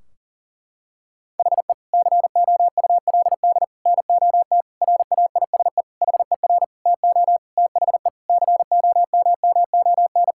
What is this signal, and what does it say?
Morse code, heard as a keyed tone, about 40 wpm, 700 Hz. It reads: HE COULD NOT RAISE HER TO THE COMMON
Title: Sentences from Top 1600 Words 40wpm